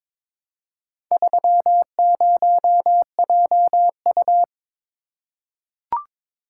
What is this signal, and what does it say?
Morse code, heard as a keyed tone, about 22 wpm, 700 Hz. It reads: 30JU
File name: practice / 22wpm / Four Letter-Number Combinations - 22wpm